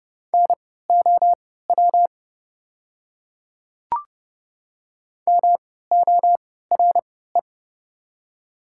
Morse code, heard as a keyed tone, about 30 wpm, spaced as for 17 wpm, 690 Hz. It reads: NOW MORE
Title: 3x Character Spacing - Top 200 Words - 30wpm